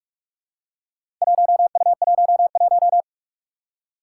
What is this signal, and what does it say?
Morse code, heard as a keyed tone, about 44 wpm, 700 Hz. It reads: 1U11
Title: Four Challenging Letter-Number Combinations 45wpm